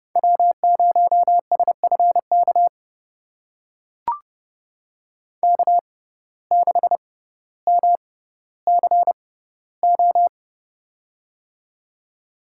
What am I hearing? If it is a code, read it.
Morse code, heard as a keyed tone, about 30 wpm, spaced as for 15 wpm, 690 Hz. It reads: W0SFK K6MCO
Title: USA Call Signs - 5 Characters - 6x Character Spacing - 30wpm